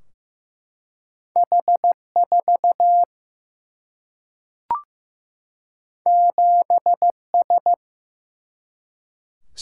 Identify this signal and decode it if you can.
Morse code, heard as a keyed tone, about 15 wpm, 700 Hz. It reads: H 4 7 S
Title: Single-Letter Single-Number 15wpm